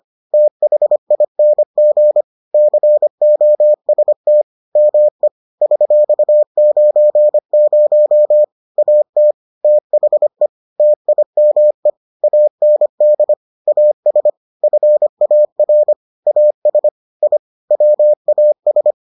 Morse code, heard as a keyed tone, about 25 wpm, 600 Hz. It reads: THING COST ME $90 AT THE TIME AND AS FAR AS I WAS